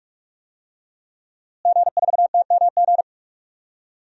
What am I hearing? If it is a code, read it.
Morse code, heard as a keyed tone, about 46 wpm, 700 Hz. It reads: G4TMG
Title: DXCC-100 Call Signs - 5 Characters - Repeated 2x - 45wpm